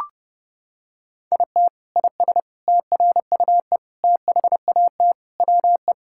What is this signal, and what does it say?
Morse code, heard as a keyed tone, about 32 wpm, 700 Hz. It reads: IT IS TRUE THAT WE
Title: Sentences from Top 1600 Words 30wpm